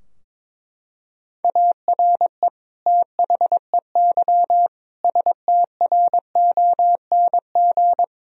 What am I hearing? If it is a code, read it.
Morse code, heard as a keyed tone, about 22 wpm, 700 Hz. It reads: ARE THEY STRONG
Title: Sentences from Top 1000 Words - Review Part 3 - 22wpm